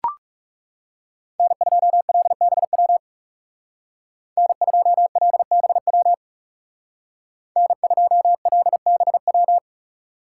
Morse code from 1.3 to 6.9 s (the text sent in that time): N2LBW N2LBW